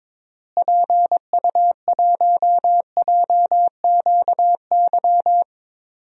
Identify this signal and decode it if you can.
Morse code, heard as a keyed tone, about 21 wpm, 700 Hz. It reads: PU1JQY
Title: DXCC-100 Call Signs - 6 Characters - Repeated 3x - 22wpm